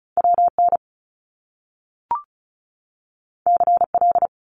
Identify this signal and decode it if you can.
Morse code, heard as a keyed tone, about 36 wpm, 700 Hz. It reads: WN CL
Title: Top Two Letter Combinations in Words 35wpm